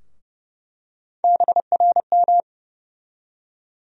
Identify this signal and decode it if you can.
Morse code, heard as a keyed tone, about 29 wpm, 700 Hz. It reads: BRM
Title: USA Call Sign Postfixes - 3 Letters 30wpm